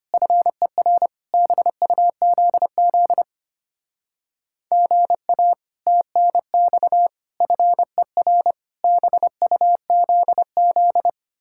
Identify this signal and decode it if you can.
Morse code, heard as a keyed tone, about 33 wpm, 700 Hz. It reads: FER BUZZ GA TNX FER BUZZ